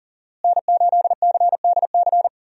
Morse code, heard as a keyed tone, about 40 wpm, 700 Hz. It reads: N8CDC
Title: USA Call Signs - 5 Characters - Repeated 2x - 40wpm